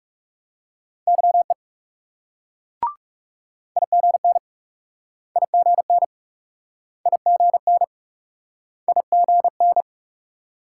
Morse code, heard as a keyed tone, about 39 wpm, 700 Hz. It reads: YE IGN IGN IGN IGN